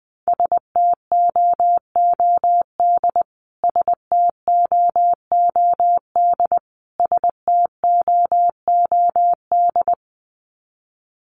Morse code, heard as a keyed tone, about 20 wpm, 700 Hz. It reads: STOOD STOOD STOOD
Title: Top 500 Words Repeated 20wpm